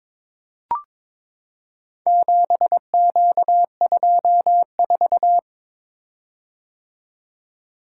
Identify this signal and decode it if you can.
Morse code, heard as a keyed tone, about 22 wpm, 700 Hz. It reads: 7Q24
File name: practice / 22wpm / Four Letter-Number Combinations - 22wpm